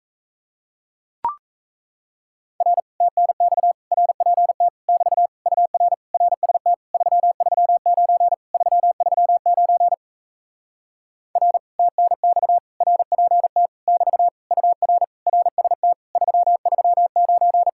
Morse code, heard as a keyed tone, about 37 wpm, 700 Hz. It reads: R TNX RPT <BT> UR RST 339 339 R TNX RPT <BT> UR RST 339